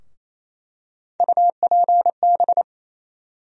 Morse code, heard as a keyed tone, about 28 wpm, 700 Hz. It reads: UPB
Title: USA Call Sign Postfixes - 3 Letters 28wpm